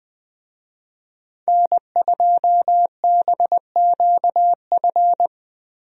Morse code, heard as a keyed tone, about 20 wpm, 700 Hz. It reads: N2BQF
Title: USA Call Signs - 5 Characters - Repeated 3x - 20wpm